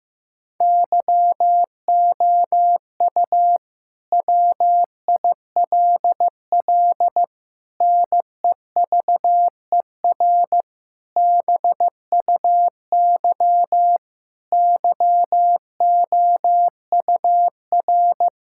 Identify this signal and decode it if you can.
Morse code, heard as a keyed tone, about 15 wpm, 700 Hz. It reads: YOU WILL NEVER BUY YOUR
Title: Sentences from Top 1000 Words - Review Part 5 - 15wpm